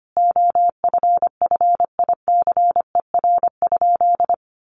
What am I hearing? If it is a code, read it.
Morse code, heard as a keyed tone, about 25 wpm, 700 Hz. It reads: OFFICER?
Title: Sentences from Top 1200 Words 25wpm